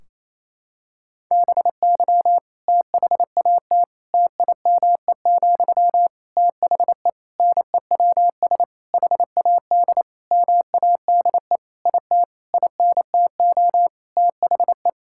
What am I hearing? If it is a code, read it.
Morse code, heard as a keyed tone, about 28 wpm, 700 Hz. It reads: BY THAT TIME, THE NEWS HAD MADE IT INTO THE
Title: Sentences from Top 1600 Words 28wpm